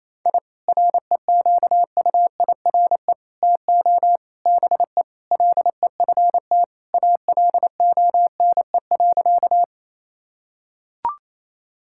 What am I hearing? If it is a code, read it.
Morse code, heard as a keyed tone, about 28 wpm, 700 Hz. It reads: I REQUIRE TO BE LEFT ALONE.
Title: Sentences from Top 1000 Words - Review Part 6 - 28wpm